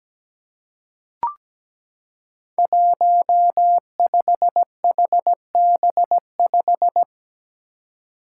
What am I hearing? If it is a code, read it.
Morse code, heard as a keyed tone, about 17 wpm, 700 Hz. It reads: 15HB5
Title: Five Challenging Letter-Number Combinations 17wpm